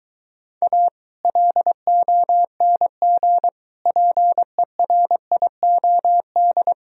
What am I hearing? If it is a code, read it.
Morse code, heard as a keyed tone, about 24 wpm, 700 Hz. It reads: A LONG PERIOD